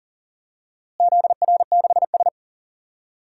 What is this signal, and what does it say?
Morse code, heard as a keyed tone, about 39 wpm, 700 Hz. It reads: ZR6S
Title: DXCC-100 Call Signs - 4 Characters - Repeated 3x - 40wpm